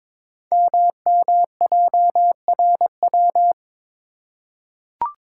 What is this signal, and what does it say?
Morse code, heard as a keyed tone, about 23 wpm, 700 Hz. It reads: MMJRW
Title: Code Groups - 5 Letters - 22wpm